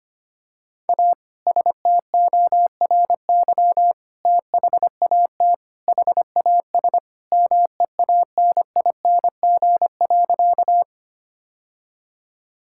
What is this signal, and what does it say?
Morse code, heard as a keyed tone, about 25 wpm, 700 Hz. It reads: A STORY THAT HAS MEANING.
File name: practice / 25wpm / Sentences with 2-Word Phrases - 25wpm